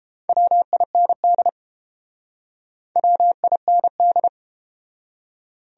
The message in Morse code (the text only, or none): WIND WIND